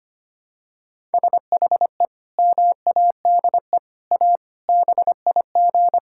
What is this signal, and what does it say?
Morse code, heard as a keyed tone, about 25 wpm, 700 Hz. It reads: SHE MADE A BIG
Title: Sentences from Top 1000 Words - Review Part 6 - 25wpm